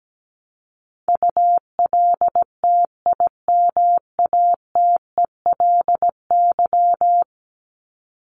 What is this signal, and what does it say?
Morse code, heard as a keyed tone, about 17 wpm, 700 Hz. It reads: ULTIMATELY
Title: Top 1001-1250 Words 17wpm